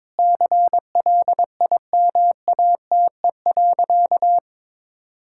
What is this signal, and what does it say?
Morse code, heard as a keyed tone, about 22 wpm, 700 Hz. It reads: CLIMATE.
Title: Sentences from Top 1600 Words 22wpm